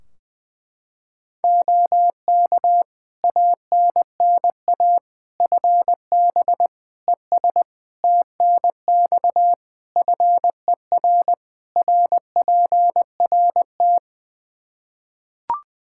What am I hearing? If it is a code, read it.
Morse code, heard as a keyed tone, about 20 wpm, 700 Hz. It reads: OK ANNA FB ES TNX FER RPRT
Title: Practice QSO - Exchange 2 - Info - 20wpm